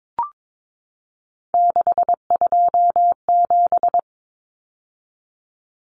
Morse code, heard as a keyed tone, about 22 wpm, 700 Hz. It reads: 627
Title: Numbers - 3-Digit 22wpm